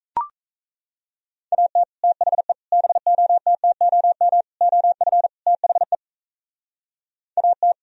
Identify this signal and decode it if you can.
Morse code, heard as a keyed tone, about 33 wpm, 700 Hz. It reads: AT THE BOTTOM OF THE AT
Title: Speed-Racing - Sets of 5 Words 28wpm